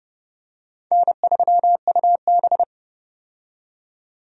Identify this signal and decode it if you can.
Morse code, heard as a keyed tone, about 32 wpm, 700 Hz. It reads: N3UB
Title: USA Call Signs - 4 Characters - 30wpm